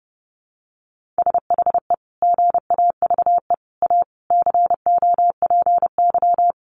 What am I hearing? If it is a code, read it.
Morse code, heard as a keyed tone, about 30 wpm, 700 Hz. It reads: SHE GAVE A COPY
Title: Sentences from Top 1400 Words 30wpm